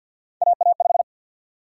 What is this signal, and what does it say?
Morse code, heard as a keyed tone, about 51 wpm, 700 Hz. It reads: AA5
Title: USA Call Sign Prefixes - 50wpm